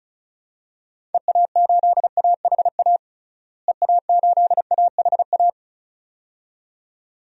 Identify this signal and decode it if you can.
Morse code, heard as a keyed tone, about 36 wpm, 700 Hz. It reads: EA8AHA EA8AHA
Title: DXCC-100 Call Signs - 6 Characters - Repeated 2x - 35wpm